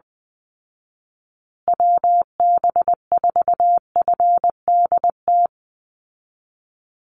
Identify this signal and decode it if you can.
Morse code, heard as a keyed tone, about 20 wpm, 700 Hz. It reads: WB4FDT
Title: Call Signs Repeated 2x - 20wpm